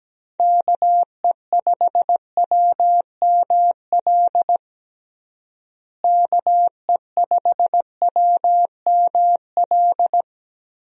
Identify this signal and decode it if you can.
Morse code, heard as a keyed tone, about 17 wpm, 700 Hz. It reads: KE5WML KE5WML
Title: USA Call Signs - 6 Characters - Repeated 2x - 17wpm